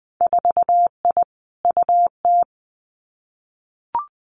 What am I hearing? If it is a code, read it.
Morse code, heard as a keyed tone, about 20 wpm, 700 Hz. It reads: 4I UT
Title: Winter Field Day Exchanges 20wpm